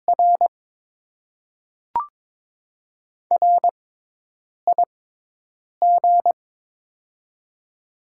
Morse code, heard as a keyed tone, about 22 wpm, spaced as for 8 wpm, 700 Hz. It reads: R RIG
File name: Top 20 QSO Elements - 6x Character Spacing - 22wpm